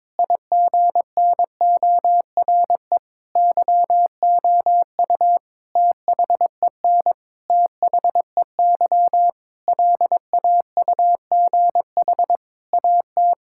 Morse code, heard as a keyed tone, about 22 wpm, 700 Hz. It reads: IGNORE YOU THEN THEY LAUGH AT
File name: practice / 22wpm / Sentences with 8-Word Phrases - 22wpm